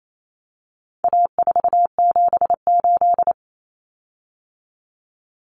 Morse code, heard as a keyed tone, about 28 wpm, 700 Hz. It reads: A478
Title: Common Cut Numbers - 4-Digit 28wpm